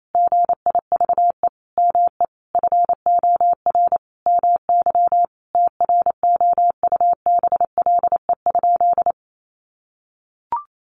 Morse code, heard as a keyed tone, about 28 wpm, 700 Hz. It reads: GIVE ME FOR MY TROUBLE?
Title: Sentences from Top 1200 Words 28wpm